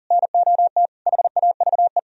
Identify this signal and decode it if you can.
Morse code, heard as a keyed tone, about 54 wpm, spaced as for 47 wpm, 700 Hz. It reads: NOT HAVE
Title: Speed-Racing - Sets of 3 Words 40wpm